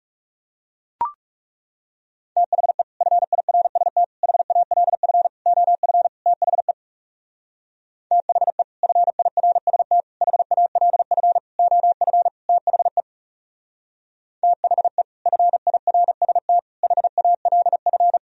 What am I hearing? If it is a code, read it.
Morse code, heard as a keyed tone, about 37 wpm, 700 Hz. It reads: THE FIRST HALF OF THE THE FIRST HALF OF THE THE FIRST HALF
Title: Speed-Racing - Sets of 5 Words 30wpm